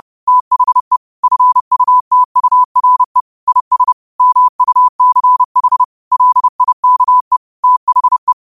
Morse code, heard as a keyed tone, about 30 wpm, 1000 Hz. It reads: THE FUTURE IS MUCH LIKE THE
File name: qotdfiles / 30wpm